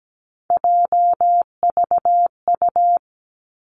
Morse code, heard as a keyed tone, about 17 wpm, 700 Hz. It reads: JVU
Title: Three Challenging Letter-Number Combinations 17wpm